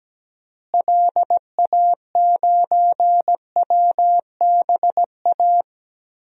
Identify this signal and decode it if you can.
Morse code, heard as a keyed tone, about 17 wpm, 700 Hz. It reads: LA9WBA
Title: DXCC-100 Call Signs - 6 Characters - Repeated 3x - 17wpm